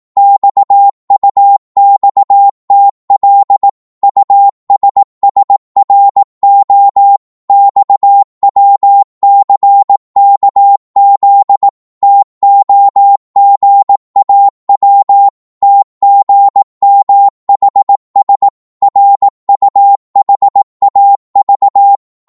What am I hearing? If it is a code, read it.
Morse code, heard as a keyed tone, about 18 wpm, 800 Hz. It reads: XUXTL USSRO XWCKZ TOGAW TGMHS RUHAV